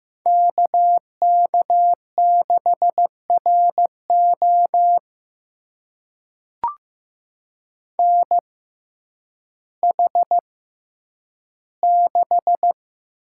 Morse code, heard as a keyed tone, about 15 wpm, spaced as for 9 wpm, 700 Hz. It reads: KK6RO NH6
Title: USA Call Signs - 5 Characters - 6x Character Spacing - 15wpm